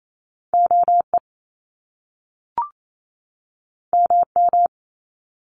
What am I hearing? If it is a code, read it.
Morse code, heard as a keyed tone, about 28 wpm, 700 Hz. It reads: OE MM
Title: Top Two Letter Combinations in Words 28wpm